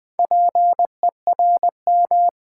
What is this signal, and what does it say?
Morse code, heard as a keyed tone, about 20 wpm, 700 Hz. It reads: PERM
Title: Build A Word - 4 Letters At A Time - 10-Letter Words - 20wpm